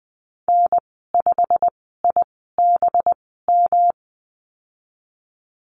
Morse code, heard as a keyed tone, about 20 wpm, spaced as for 15 wpm, 700 Hz. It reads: N5IBM
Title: 186 Lesson 30g - Letter b - Call Signs 20wpm